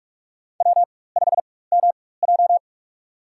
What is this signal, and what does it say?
Morse code, heard as a keyed tone, about 46 wpm, spaced as for 21 wpm, 700 Hz. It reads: W5MJ
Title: USA Call Signs - 4 Characters - 4x Character Spacing - 45wpm